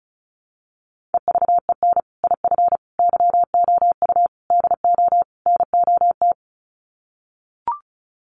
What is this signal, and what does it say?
Morse code, heard as a keyed tone, about 34 wpm, 700 Hz. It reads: EVEN IF YOU DO NOT
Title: Sets of 5 Words Encore 35wpm